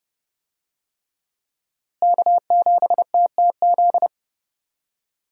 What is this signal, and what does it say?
Morse code, heard as a keyed tone, about 30 wpm, 700 Hz. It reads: K7TTZ
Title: USA Call Signs - 5 Characters - Repeated 3x - 30wpm